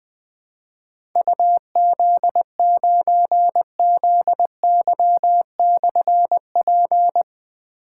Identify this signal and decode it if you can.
Morse code, heard as a keyed tone, about 20 wpm, 700 Hz. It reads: UZ9ZY/P
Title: Difficult Call Signs 20wpm